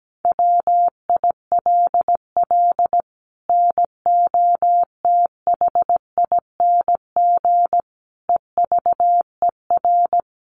Morse code, heard as a keyed tone, about 17 wpm, 700 Hz. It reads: WILL NOTHING EVER
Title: Sentences from Top 1200 Words 17wpm